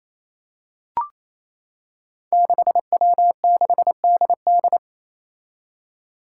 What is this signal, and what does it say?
Morse code, heard as a keyed tone, about 28 wpm, 700 Hz. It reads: 6W6DD
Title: Five Challenging Letter-Number Combinations 28wpm